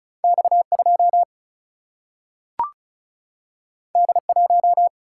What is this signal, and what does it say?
Morse code, heard as a keyed tone, about 36 wpm, 700 Hz. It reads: X2 D1
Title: Single-Letter Single-Number 35wpm